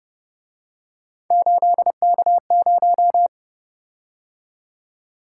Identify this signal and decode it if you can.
Morse code, heard as a keyed tone, about 29 wpm, 700 Hz. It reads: 8K0
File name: Speed-Racing - Three Letter-Number Combinations - 25wpm